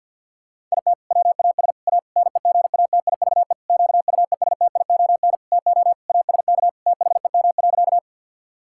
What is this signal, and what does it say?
Morse code, heard as a keyed tone, about 52 wpm, 700 Hz. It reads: IT WAS A NEGATIVE QUESTION TO ASK THEM.
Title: Sentences from Top 1000 Words - Review Part 6 - 50wpm